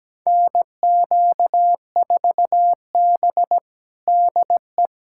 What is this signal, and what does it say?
Morse code, heard as a keyed tone, about 17 wpm, 700 Hz. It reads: NQ4B DE
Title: USA Callsign DE Callsign - 4 Characters - 17wpm